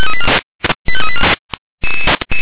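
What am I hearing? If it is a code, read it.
Five-tone selective call (ZVEI), heard as quick runs of five tones, tones of about 70 ms. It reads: 85384
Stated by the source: ZVEI five-tone call received on SDR.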